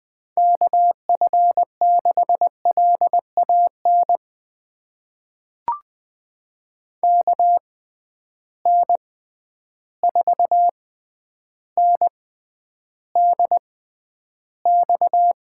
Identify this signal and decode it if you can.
Morse code, heard as a keyed tone, about 20 wpm, spaced as for 10 wpm, 700 Hz. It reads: KF6LAN KN4NDX
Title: USA Call Signs - 6 Characters - 6x Character Spacing - 20wpm